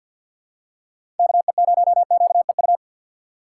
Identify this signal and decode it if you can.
Morse code, heard as a keyed tone, about 49 wpm, 700 Hz. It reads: KE0QEU
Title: USA Call Signs - 3x Character Spacing - 50wpm